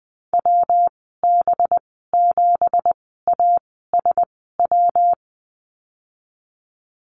Morse code, heard as a keyed tone, about 20 wpm, spaced as for 14 wpm, 700 Hz. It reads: WB7ASW
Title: 193 Lesson 31g - Letter v - Call Signs 20wpm